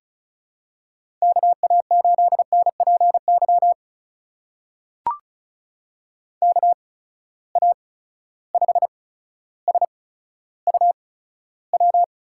Morse code, heard as a keyed tone, about 36 wpm, spaced as for 15 wpm, 700 Hz. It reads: KA8NPY KA5SUW
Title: USA Call Signs - 6 Characters - 8x Character Spacing - 35wpm